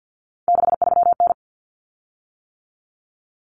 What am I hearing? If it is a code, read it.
Morse code, heard as a keyed tone, about 50 wpm, 700 Hz. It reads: 63N
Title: Common Cut Numbers - 3-Digit 50wpm